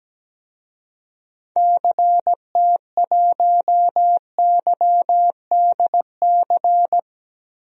Morse code, heard as a keyed tone, about 17 wpm, 700 Hz. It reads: CT1YDC
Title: DXCC-100 Call Signs - 6 Characters - Repeated 3x - 17wpm